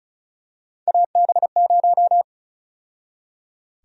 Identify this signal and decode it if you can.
Morse code, heard as a keyed tone, about 36 wpm, 700 Hz. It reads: AB0
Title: USA Call Sign Prefixes - 35wpm